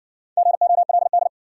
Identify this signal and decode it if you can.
Morse code, heard as a keyed tone, about 57 wpm, 700 Hz. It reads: GOLD